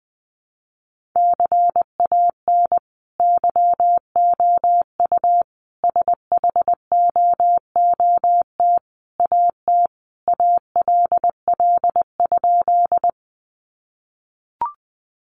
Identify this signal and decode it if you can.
Morse code, heard as a keyed tone, about 19 wpm, 700 Hz. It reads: CAN YOU SHOOT AT ALL?
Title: Sentences from Top 1200 Words 20wpm